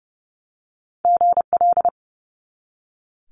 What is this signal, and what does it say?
Morse code, heard as a keyed tone, about 30 wpm, 700 Hz. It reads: GL
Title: Multi-letter Phonemes 30wpm